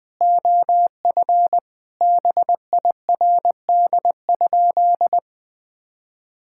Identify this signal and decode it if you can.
Morse code, heard as a keyed tone, about 20 wpm, 700 Hz. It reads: OF BIRD?
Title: Sentences with 8-Word Phrases - 20wpm